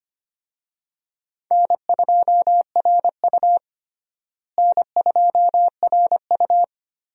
Morse code, heard as a keyed tone, about 25 wpm, 700 Hz. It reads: N2RU N2RU
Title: USA Call Signs - 4 Characters - Repeated 3x - 25wpm